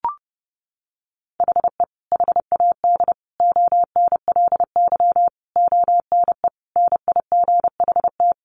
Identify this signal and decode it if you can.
Morse code, heard as a keyed tone, about 30 wpm, 700 Hz. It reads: HE HAD ONLY ONE NIGHT
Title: Sentences from Top 300 Words 30wpm